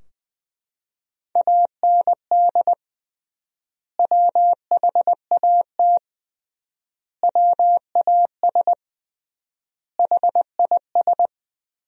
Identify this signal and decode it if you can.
Morse code, heard as a keyed tone, about 20 wpm, 700 Hz. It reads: AND WHAT WAS HIS
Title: 3x Word-Spacing - Sentences from Top 200 Words - 20wpm